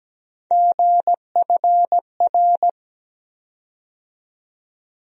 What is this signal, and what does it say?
Morse code, heard as a keyed tone, about 17 wpm, 700 Hz. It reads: GFR
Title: USA Call Sign Postfixes - 3 Letters 17wpm